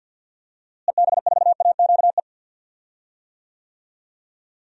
Morse code, heard as a keyed tone, about 47 wpm, 700 Hz. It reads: EB4AQE
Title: DXCC-100 Call Signs - 6 Characters - Repeated 2x - 50wpm